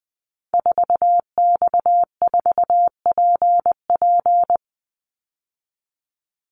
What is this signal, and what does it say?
Morse code, heard as a keyed tone, about 20 wpm, 700 Hz. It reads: 4X4PP
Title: Call Signs - 20wpm